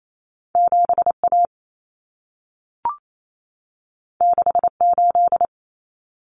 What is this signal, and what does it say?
Morse code, heard as a keyed tone, about 28 wpm, 700 Hz. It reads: 7A 68
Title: Common Cut Numbers - 2-Digit 28wpm